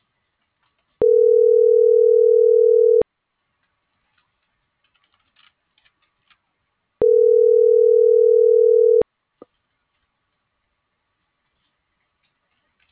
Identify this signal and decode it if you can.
Ringback tone (North America), 440 + 480 Hz.